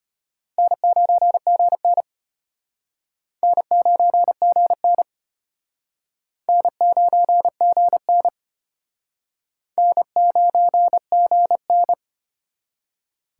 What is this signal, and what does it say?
Morse code, heard as a keyed tone, about 33 wpm, 700 Hz. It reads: N9GN N9GN N9GN N9GN